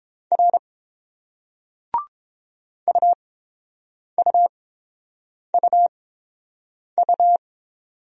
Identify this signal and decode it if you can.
Morse code, heard as a keyed tone, about 30 wpm, spaced as for 7 wpm, 700 Hz. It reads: R UUUU